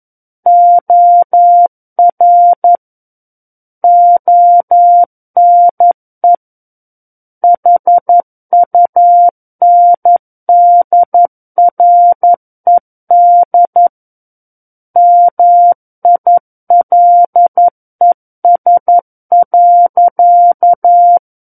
Morse code, heard as a keyed tone, about 11 wpm, 700 Hz. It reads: OR ONE HUNDRED MILES.
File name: Book One / 02-Chapter 2 - 11 WPM